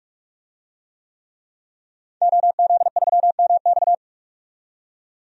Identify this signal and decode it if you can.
Morse code, heard as a keyed tone, about 46 wpm, 700 Hz. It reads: OZ3MX
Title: DXCC-100 Call Signs - 5 Characters - 45wpm